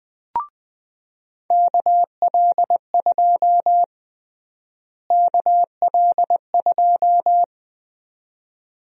Morse code, heard as a keyed tone, about 20 wpm, 700 Hz. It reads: KL2 KL2
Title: USA Call Sign Prefixes - Repeated 2x - 20wpm